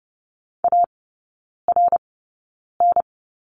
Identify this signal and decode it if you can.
Morse code, heard as a keyed tone, about 30 wpm, spaced as for 9 wpm, 700 Hz. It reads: ARN
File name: Lesson 16e - Letter c - Three Characters 30wpm